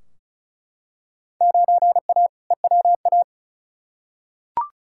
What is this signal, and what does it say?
Morse code, heard as a keyed tone, about 36 wpm, 690 Hz. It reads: 9A EWA
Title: Field Day Exchanges Encore - 35wpm